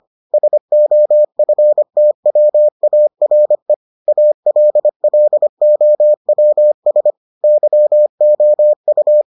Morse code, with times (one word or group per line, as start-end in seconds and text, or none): 0.3-3.7 SOFTWARE
4.1-7.1 ALLOWS
7.4-9.2 YOU